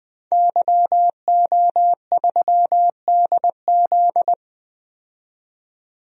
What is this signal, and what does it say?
Morse code, heard as a keyed tone, about 20 wpm, 700 Hz. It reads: YO3DZ
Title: DXCC-100 Call Signs - 5 Characters - 20wpm